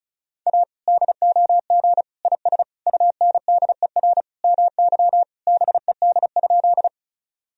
Morse code, heard as a keyed tone, about 36 wpm, 700 Hz. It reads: A DOG IS UNDER MY BED?
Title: Sentences from Top 1000 Words - Review Part 2 - 35wpm